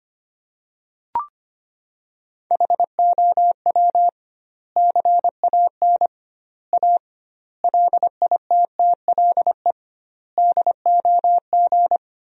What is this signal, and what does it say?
Morse code, heard as a keyed tone, about 25 wpm, 700 Hz. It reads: HOW CAN A LITTLE DOG
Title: 2x Word-Spacing - Sentences from Top 500 Words - 25wpm